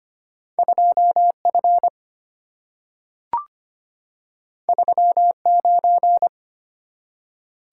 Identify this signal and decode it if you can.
Morse code, heard as a keyed tone, about 25 wpm, 700 Hz. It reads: 2F 39
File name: Two Letter-Number Combinations - 25wpm